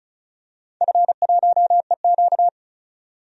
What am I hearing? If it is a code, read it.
Morse code, heard as a keyed tone, about 36 wpm, 700 Hz. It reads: F1EQ
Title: DXCC-100 Call Signs - 4 Characters - 35wpm